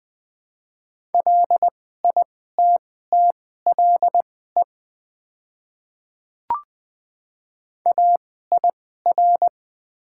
Morse code, heard as a keyed tone, about 21 wpm, spaced as for 15 wpm, 700 Hz. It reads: LITTLE AIR
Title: 2x Character Spacing - Top 500 Words - 20wpm